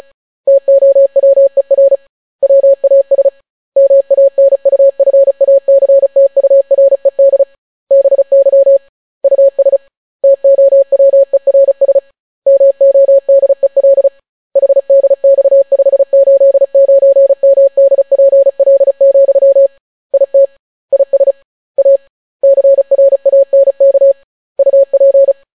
Morse code, heard as a keyed tone, about 36 wpm, 550 Hz. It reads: TOWER WAS MANUFACTURED BY US TOWERS MODEL HDX589MDPL, IT IS A CRANK UP